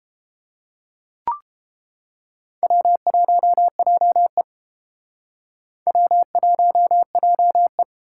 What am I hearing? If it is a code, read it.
Morse code, heard as a keyed tone, about 29 wpm, 700 Hz. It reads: W1JE W1JE